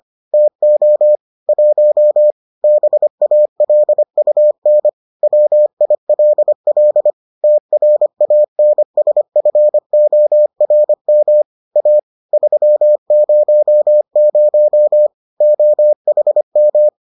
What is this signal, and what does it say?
Morse code, heard as a keyed tone, about 25 wpm, 600 Hz. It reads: TO 1 BALUN WILL TRANSFORM A 300 OHM